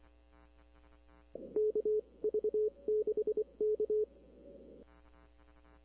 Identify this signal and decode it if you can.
Morse code, heard as a keyed tone, about 24 wpm, spaced as for 21 wpm, 430 Hz. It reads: KV6K
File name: arrl cw dx